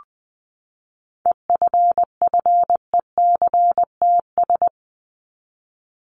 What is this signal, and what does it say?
Morse code, heard as a keyed tone, about 20 wpm, 700 Hz. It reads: EFFECTS
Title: ES:S Suffix - Top 100 Words with Base Words 20wpm